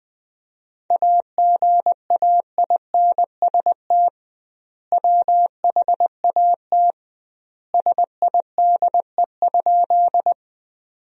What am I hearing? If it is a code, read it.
Morse code, heard as a keyed tone, about 20 wpm, 700 Hz. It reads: AGAINST WHAT SIDE?
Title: 2x Word-Spacing - Sentences from Top 400 Words - 20wpm